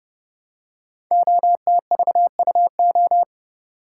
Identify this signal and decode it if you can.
Morse code, heard as a keyed tone, about 29 wpm, 700 Hz. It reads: OTVUO
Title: Code Groups - 5 Letter:Numbers - 30wpm